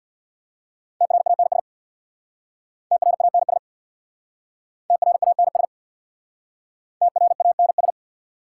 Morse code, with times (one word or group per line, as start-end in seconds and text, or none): 1.0-1.6 TRANS
2.9-3.6 TRANS
4.9-5.7 TRANS
7.0-7.9 TRANS